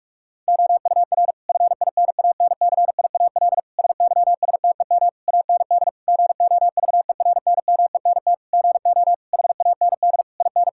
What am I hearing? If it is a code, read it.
Morse code, heard as a keyed tone, about 46 wpm, 700 Hz. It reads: OUR FINANCIAL SYSTEM AND GOVERNMENT GO HAND IN
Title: Sentences from Top 1000 Words - Review Part 5 - 45wpm